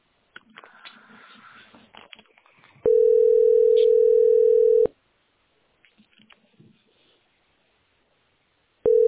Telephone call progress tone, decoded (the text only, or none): ringback tone